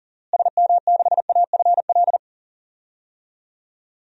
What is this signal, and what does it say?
Morse code, heard as a keyed tone, about 40 wpm, 700 Hz. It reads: SM6AFL
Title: DXCC-100 Call Signs - 6 Characters - Repeated 3x - 40wpm